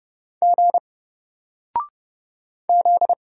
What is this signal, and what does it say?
Morse code, heard as a keyed tone, about 30 wpm, 700 Hz. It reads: G Z
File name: Single-Letter-Number-30wpm